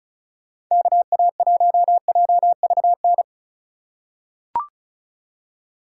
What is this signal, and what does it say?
Morse code, heard as a keyed tone, about 36 wpm, 700 Hz. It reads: KA1JVN
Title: USA Call Signs - 6 Characters - 3x Character Spacing - 35wpm